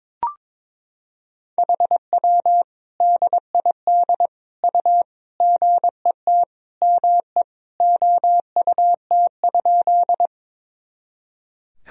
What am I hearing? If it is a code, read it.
Morse code, heard as a keyed tone, about 22 wpm, 700 Hz. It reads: HW DID U GET ME OUT?
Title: Sentences from Top 200 Words and Abbreviations 22wpm